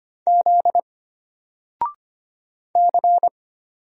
Morse code, heard as a keyed tone, about 25 wpm, 700 Hz. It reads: Z C
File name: Single-Letter-Number-25wpm